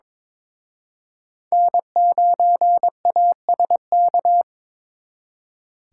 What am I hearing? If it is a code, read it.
Morse code, heard as a keyed tone, about 26 wpm, 700 Hz. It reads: N9ASK
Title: Speed Racing - USA Call Signs 22wpm